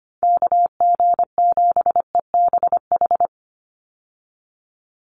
Morse code, heard as a keyed tone, about 25 wpm, 700 Hz. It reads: KG7EBH
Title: USA Call Signs 25wpm